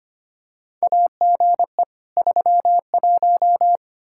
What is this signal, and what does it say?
Morse code, heard as a keyed tone, about 25 wpm, 700 Hz. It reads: AGE 31
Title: Practice QSO - Exchange 3 - More Info - 25wpm